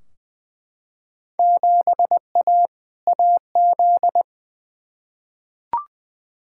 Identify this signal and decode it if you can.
Morse code, heard as a keyed tone, about 20 wpm, 700 Hz. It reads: 7A AZ
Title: Field Day Exchanges - 20wpm